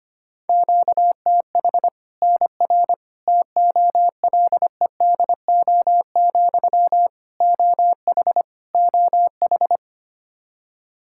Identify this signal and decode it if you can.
Morse code, heard as a keyed tone, about 25 wpm, 700 Hz. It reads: QTH NR TOLEDO, OH OH
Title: Practice QSO - Exchange 1 - Beginning - 25wpm